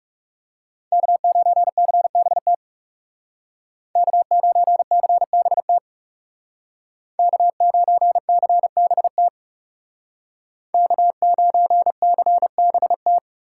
Morse code, heard as a keyed tone, about 38 wpm, 700 Hz. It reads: K9CBT K9CBT K9CBT K9CBT